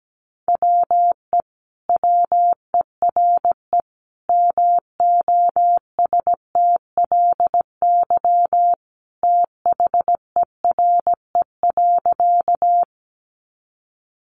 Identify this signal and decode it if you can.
Morse code, heard as a keyed tone, about 17 wpm, 700 Hz. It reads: WE WERE MOSTLY THERE.
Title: Sentences from Top 1300 Words 17wpm